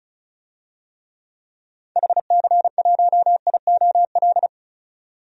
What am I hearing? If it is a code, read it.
Morse code, heard as a keyed tone, about 36 wpm, 690 Hz. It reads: HC1IOL
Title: DXCC-100 Call Signs - 6 Characters - 35wpm